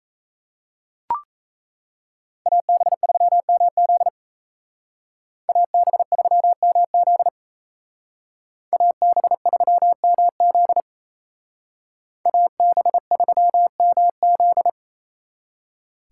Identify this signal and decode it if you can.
Morse code, heard as a keyed tone, about 33 wpm, 700 Hz. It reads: AB3MZ AB3MZ AB3MZ AB3MZ
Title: Speed Racing - USA Call Signs 28wpm